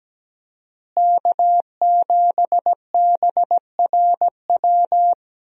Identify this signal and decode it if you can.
Morse code, heard as a keyed tone, about 17 wpm, 700 Hz. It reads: K7BRW
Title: USA Call Signs - 5 Characters - Repeated 3x - 17wpm